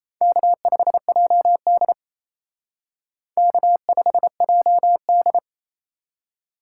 Morse code, heard as a keyed tone, about 35 wpm, 700 Hz. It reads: K5JD K5JD
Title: Speed Racing - USA 1x2 Call Signs 28wpm